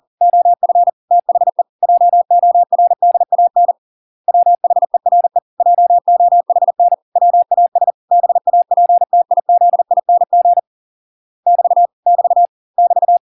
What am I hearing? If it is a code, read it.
Morse code, heard as a keyed tone, about 40 wpm, 700 Hz. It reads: OF THE JORDAN WHERE JOHN WAS BAPTIZING <BT> <BT> <BT>